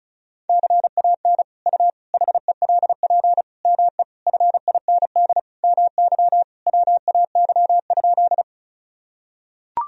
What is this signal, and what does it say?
Morse code, heard as a keyed tone, about 36 wpm, 700 Hz. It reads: CAN U HELP ME FIND MY WAY?
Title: Sentences from Top 200 Words and Abbreviations 35wpm